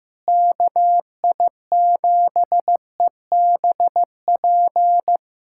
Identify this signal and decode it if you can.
Morse code, heard as a keyed tone, about 15 wpm, 700 Hz. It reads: KI7EBP
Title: USA Callsign DE Callsign - 6 Characters - 15wpm